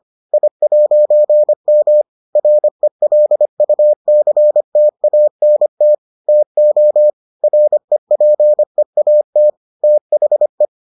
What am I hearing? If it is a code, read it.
Morse code, heard as a keyed tone, about 25 wpm, 600 Hz. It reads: I'M RELUCTANT TO REPEAT THE